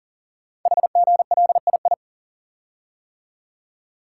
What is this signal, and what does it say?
Morse code, heard as a keyed tone, about 40 wpm, 700 Hz. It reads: HGLII
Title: Code Groups - 5 Letters - 40wpm